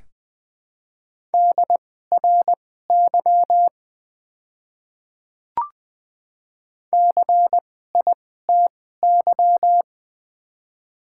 Morse code, heard as a keyed tone, about 20 wpm, spaced as for 14 wpm, 700 Hz. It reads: DRY CITY
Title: 2x Character Spacing - Top 500 Words - 20wpm